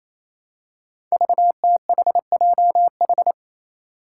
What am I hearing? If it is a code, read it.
Morse code, heard as a keyed tone, about 27 wpm, 700 Hz. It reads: VTHJH
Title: Five Challenging Letter-Number Combinations 28wpm